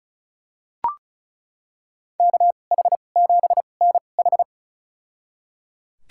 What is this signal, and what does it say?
Morse code, heard as a keyed tone, about 34 wpm, spaced as for 25 wpm, 700 Hz. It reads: KH7NH
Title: USA Call Signs - 5 Characters - 2x Character Spacing - 35wpm